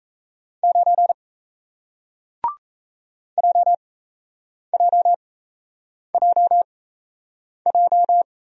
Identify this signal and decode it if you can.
Morse code, heard as a keyed tone, about 38 wpm, spaced as for 8 wpm, 700 Hz. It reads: 9 JJJJ